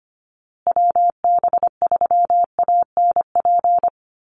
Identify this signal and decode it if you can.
Morse code, heard as a keyed tone, about 25 wpm, 700 Hz. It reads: WB3ANP
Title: USA Call Signs Repeated 2x 25wpm